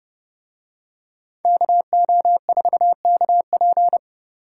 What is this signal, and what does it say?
Morse code, heard as a keyed tone, about 29 wpm, 700 Hz. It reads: KO4KP